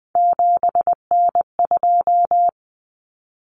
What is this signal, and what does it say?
Morse code, heard as a keyed tone, about 20 wpm, 700 Hz. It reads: 7N2
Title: Common Cut Numbers - 3-Digit 20wpm